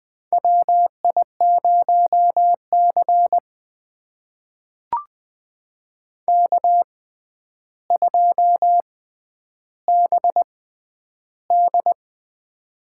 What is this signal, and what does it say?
Morse code, heard as a keyed tone, about 20 wpm, spaced as for 10 wpm, 700 Hz. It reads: WI0C K2BD
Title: USA Call Signs - 4 Characters - 6x Character Spacing - 20wpm